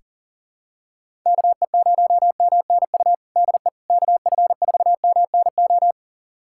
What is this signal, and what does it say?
Morse code, heard as a keyed tone, about 40 wpm, 700 Hz. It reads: KE0MNU DE KF4MNO
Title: USA Callsign DE Callsign - 6 Characters - 40wpm